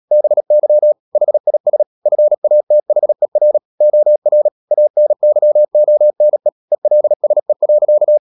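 Morse code, heard as a keyed tone, about 37 wpm, 600 Hz. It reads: BY HIS FATHER OR ANYONE ELSE.
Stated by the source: Morse code practice at 35–39 wpm sending English text of Ken Follett's novel Never, from CW med Gnister.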